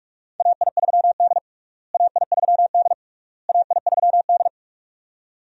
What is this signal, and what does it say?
Morse code, heard as a keyed tone, about 46 wpm, 700 Hz. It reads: AI3D AI3D AI3D
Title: USA Call Signs - 4 Characters - Repeated 3x - 45wpm